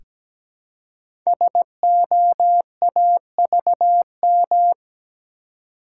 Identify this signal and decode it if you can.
Morse code, heard as a keyed tone, about 17 wpm, 700 Hz. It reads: SOAVM
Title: Code Groups - 5 Letters - 17wpm